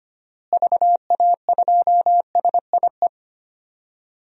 Morse code, heard as a keyed tone, about 25 wpm, 700 Hz. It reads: VA2SIE
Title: DXCC-100 Call Signs - 6 Characters - Repeated 3x - 25wpm